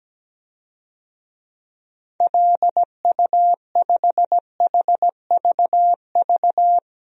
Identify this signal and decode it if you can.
Morse code, heard as a keyed tone, about 17 wpm, 700 Hz. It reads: LU5HVV
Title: DXCC-100 Call Signs - 6 Characters - 17wpm